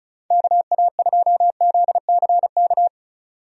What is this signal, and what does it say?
Morse code, heard as a keyed tone, about 36 wpm, 700 Hz. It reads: KA2ZCK
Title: USA Call Signs - 6 Characters - 35wpm